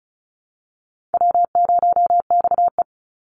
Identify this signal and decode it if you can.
Morse code, heard as a keyed tone, about 34 wpm, 700 Hz. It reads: W0XE
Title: Easy Call Signs Repeated 3x - 35wpm